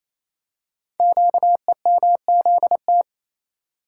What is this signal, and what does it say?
Morse code, heard as a keyed tone, about 28 wpm, 700 Hz. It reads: QEMZT
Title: Code Groups - 5 Letters - 28wpm